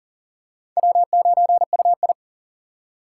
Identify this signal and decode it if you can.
Morse code, heard as a keyed tone, about 40 wpm, 700 Hz. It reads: W9UI
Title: USA Call Signs - 4 Characters - 4x Character Spacing - 40wpm